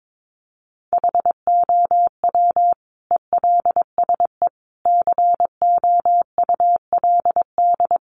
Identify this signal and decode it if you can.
Morse code, heard as a keyed tone, about 22 wpm, 700 Hz. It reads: HOW ELSE COULD
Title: Sentences from Top 1300 Words 22wpm